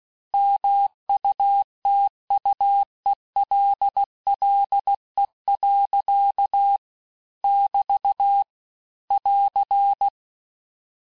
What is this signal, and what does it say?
Morse code, heard as a keyed tone, about 16 wpm, 800 Hz. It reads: MUTUELLE. <BT> <AR>
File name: texte-clair-vitesse-15mpm-puis-20mpm